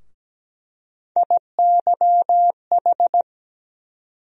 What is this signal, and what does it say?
Morse code, heard as a keyed tone, about 17 wpm, 700 Hz. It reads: IYH
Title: USA Call Sign Postfixes - 3 Letters 17wpm